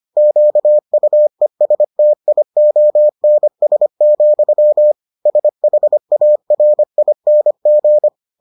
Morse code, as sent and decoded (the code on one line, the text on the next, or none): --.- ..- . ... - .. --- -. ... --..-- / ... .... .- .-. .. -. --.
QUESTIONS, SHARING